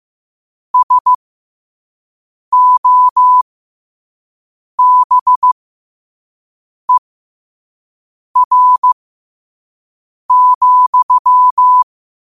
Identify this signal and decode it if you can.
Morse code, heard as a keyed tone, about 15 wpm, spaced as for 5 wpm, 1000 Hz. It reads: SOBER,
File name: qotdfiles / 5wpm